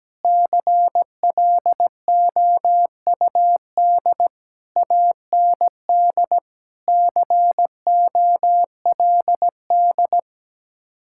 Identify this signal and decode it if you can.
Morse code, heard as a keyed tone, about 17 wpm, 700 Hz. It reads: CLOUD AND COLD
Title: Speed-Racing Common QSO Phrases - 17wpm